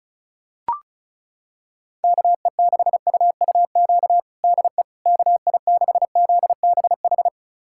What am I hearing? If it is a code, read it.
Morse code, heard as a keyed tone, about 36 wpm, 700 Hz. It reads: KE6UUQ DE KI6ZBH
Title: USA Callsign DE Callsign 35wpm